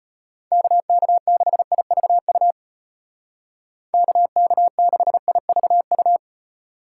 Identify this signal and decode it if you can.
Morse code, heard as a keyed tone, about 34 wpm, 700 Hz. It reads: KK6IVU KK6IVU